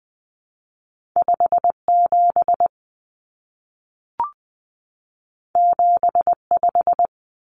Morse code, heard as a keyed tone, about 20 wpm, 700 Hz. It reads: 57 75
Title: Common Cut Numbers - 2-Digit 20wpm